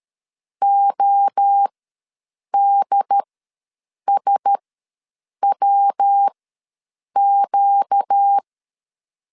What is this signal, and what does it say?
Morse code, heard as a keyed tone, about 13 wpm, spaced as for 7 wpm, 800 Hz. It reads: ODSWQ
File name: LEZIONI / 8 YJ45 / 8 YJ4 ws 12wpm 3249